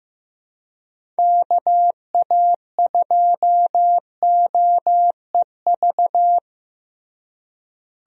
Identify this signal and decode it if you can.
Morse code, heard as a keyed tone, about 15 wpm, 700 Hz. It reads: KA2OEV
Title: USA Call Signs - 6 Characters - Repeated 2x - 15wpm